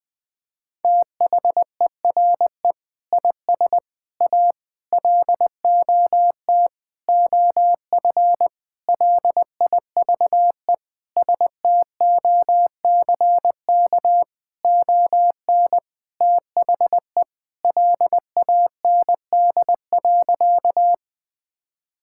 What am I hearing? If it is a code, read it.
Morse code, heard as a keyed tone, about 20 wpm, 700 Hz. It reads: THERE IS A LOT OF LIVE STOCK ON THE LAND.
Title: Sentences from Top 1000 Words - Review Part 5 - 20wpm